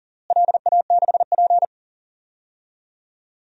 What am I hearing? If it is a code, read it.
Morse code, heard as a keyed tone, about 37 wpm, 700 Hz. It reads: LA6P
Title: DXCC-100 Call Signs - 4 Characters - Repeated 2x - 40wpm